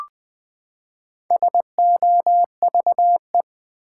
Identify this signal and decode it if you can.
Morse code, heard as a keyed tone, about 20 wpm, 700 Hz. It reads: SOVE
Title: Build A Word - 4 Letters At A Time - 11-Letter Words - 20wpm